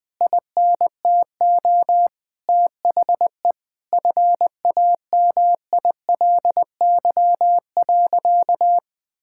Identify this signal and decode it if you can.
Morse code, heard as a keyed tone, about 21 wpm, 700 Hz. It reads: INTO THE FAMILY.
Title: Sentences from Top 1600 Words 20wpm